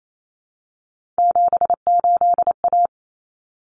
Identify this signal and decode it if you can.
Morse code, heard as a keyed tone, about 28 wpm, 700 Hz. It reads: 78A
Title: Common Cut Numbers - 3-Digit 28wpm